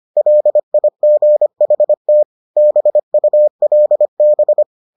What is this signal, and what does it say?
Morse code, heard as a keyed tone, about 25 wpm, 600 Hz. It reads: LIGHT BULB